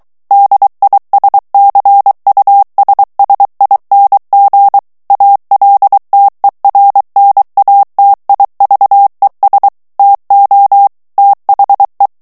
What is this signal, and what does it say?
Morse code, heard as a keyed tone, about 23 wpm, 800 Hz. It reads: DISCUSSING ALTERNATIVES TO THE
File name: ANGLAIS texte clair 3 vitesse 22 mots minute